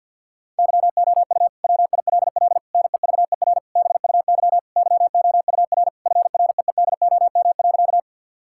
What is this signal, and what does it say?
Morse code, heard as a keyed tone, about 50 wpm, 700 Hz. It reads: YOU WILL NEVER BUY YOUR FREEDOM.
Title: Sentences from Top 1000 Words - Review Part 5 - 50wpm